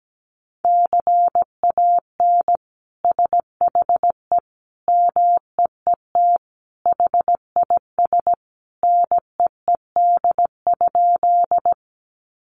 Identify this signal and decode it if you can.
Morse code, heard as a keyed tone, about 17 wpm, 700 Hz. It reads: CAN SHE MEET HIS NEED?
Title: Sentences from Top 1100 Words 17wpm